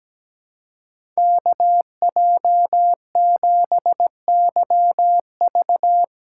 Morse code, heard as a keyed tone, about 17 wpm, 700 Hz. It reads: KJ7YV
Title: USA Callsign DE Callsign - 5 Characters - 17wpm